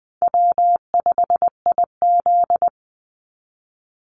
Morse code, heard as a keyed tone, about 20 wpm, 700 Hz. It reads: W5IZ
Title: Call Signs Repeated 3x - 20wpm